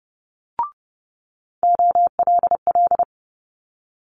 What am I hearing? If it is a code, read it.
Morse code, heard as a keyed tone, about 30 wpm, 700 Hz. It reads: OLL
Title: Multi-letter Phonemes Quick Fire 30wpm